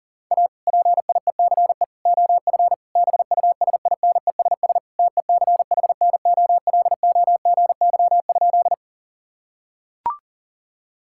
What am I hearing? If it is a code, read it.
Morse code, heard as a keyed tone, about 40 wpm, 690 Hz. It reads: A PIECE OF BUSINESS TECHNOLOGY?
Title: Sentences from Top 1000 Words - Review Part 5 - 40wpm